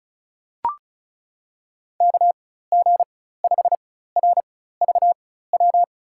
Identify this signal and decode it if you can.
Morse code, heard as a keyed tone, about 34 wpm, spaced as for 16 wpm, 700 Hz. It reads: KG5RVW
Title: USA Call Signs - 4x Character Spacing - 35wpm